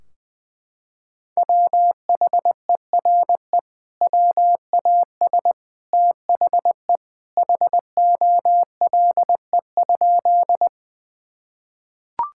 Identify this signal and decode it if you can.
Morse code, heard as a keyed tone, about 20 wpm, 700 Hz. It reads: WHERE WAS THE HOLE?
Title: Sentences from Top 1600 Words 20wpm